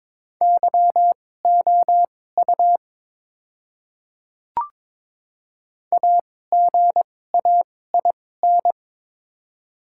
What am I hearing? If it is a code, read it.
Morse code, heard as a keyed tone, about 22 wpm, spaced as for 16 wpm, 700 Hz. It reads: YOU AGAIN
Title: 2x Character Spacing - Top 200 Words - 22wpm